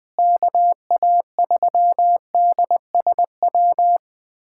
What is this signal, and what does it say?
Morse code, heard as a keyed tone, about 21 wpm, 700 Hz. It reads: KA3DSW
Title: USA Callsign DE Callsign - 6 Characters - 20wpm